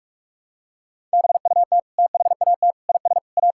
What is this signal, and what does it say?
Morse code, heard as a keyed tone, about 46 wpm, 700 Hz. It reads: BUT THAT IS A
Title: Sentences from Top 1000 Words - Review Part 6 - 45wpm